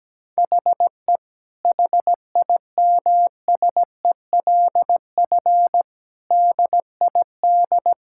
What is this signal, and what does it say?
Morse code, heard as a keyed tone, about 17 wpm, 700 Hz. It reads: HE HIMSELF DID
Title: Sentences from Top 1600 Words 17wpm